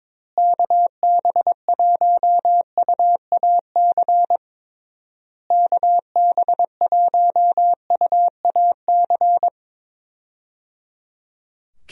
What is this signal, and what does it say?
Morse code, heard as a keyed tone, about 21 wpm, 700 Hz. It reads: KB1UAC KB1UAC
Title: USA Call Signs - 6 Characters - Repeated 3x - 22wpm